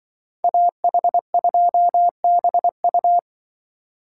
Morse code, heard as a keyed tone, about 23 wpm, 700 Hz. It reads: AH2BU
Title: Speed Racing - USA 2x2 Call Signs 20wpm